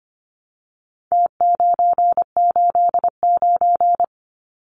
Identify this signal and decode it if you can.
Morse code, heard as a keyed tone, about 25 wpm, 700 Hz. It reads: T989
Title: Numbers - Contest Style - 4-Digit 25wpm